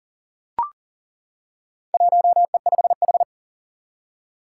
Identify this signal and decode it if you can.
Morse code, heard as a keyed tone, about 37 wpm, 700 Hz. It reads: 1E5H
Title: Four Challenging Letter-Number Combinations 40wpm